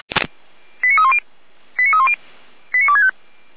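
Five-tone selective call (ZVEI), heard as quick runs of five tones, tones of about 70 ms. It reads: 89310 89310 89265
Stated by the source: Five-tone ZVEI selective call 89337 received by SDR, from the audio server of Feuerwehr Orlamünde.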